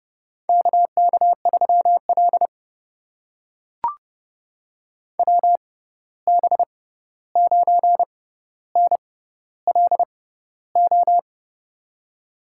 Morse code, heard as a keyed tone, about 30 wpm, spaced as for 14 wpm, 700 Hz. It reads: KK3L WB9NLO
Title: USA Call Signs - 6x Character Spacing - 30wpm